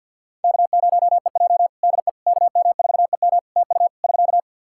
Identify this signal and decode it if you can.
Morse code, heard as a keyed tone, about 50 wpm, 700 Hz. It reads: K0EJ DE KM4EM TU <SK>
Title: Practice QSO - Exchange 4 - Ending - 50wpm